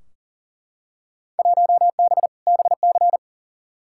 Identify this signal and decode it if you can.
Morse code, heard as a keyed tone, about 40 wpm, 700 Hz. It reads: 1B BC
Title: Field Day Exchanges - 40wpm